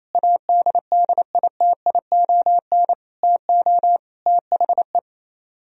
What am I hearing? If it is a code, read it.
Morse code, heard as a keyed tone, about 36 wpm, 700 Hz. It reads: ADDITION TO THE